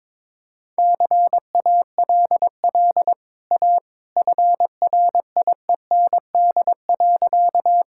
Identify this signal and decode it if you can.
Morse code, heard as a keyed tone, about 23 wpm, 700 Hz. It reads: CALL A FRIEND.
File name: practice / 22wpm / Easy Sentences with Hints - 22wpm